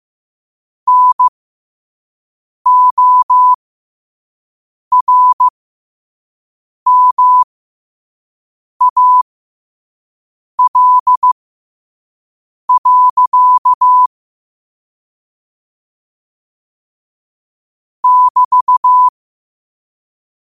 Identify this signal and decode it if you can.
Morse code, heard as a keyed tone, about 15 wpm, spaced as for 5 wpm, 1000 Hz. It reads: NORMAL. <BT>